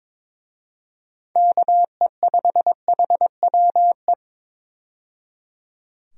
Morse code, heard as a keyed tone, about 22 wpm, 700 Hz. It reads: KE5HWE
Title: USA Call Signs - 6 Characters - 22wpm